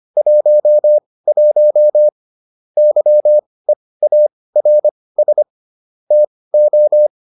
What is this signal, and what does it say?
Morse code, heard as a keyed tone, about 25 wpm, spaced as for 18 wpm, 600 Hz. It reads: 11 YEARS TO